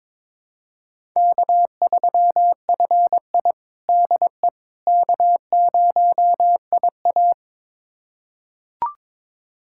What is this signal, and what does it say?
Morse code, heard as a keyed tone, about 22 wpm, 700 Hz. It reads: K3FI DE K0IA
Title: USA Callsign DE Callsign - 4 Characters - 22wpm